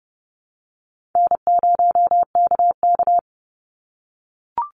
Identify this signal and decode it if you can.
Morse code, heard as a keyed tone, about 30 wpm, 700 Hz. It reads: N0KK
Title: Easy Call Signs - 30wpm